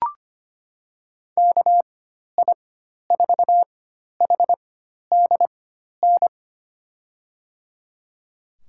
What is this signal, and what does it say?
Morse code, heard as a keyed tone, about 25 wpm, spaced as for 12 wpm, 700 Hz. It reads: KI4HDN
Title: USA Call Signs - 6 Characters - 4x Character Spacing - 25wpm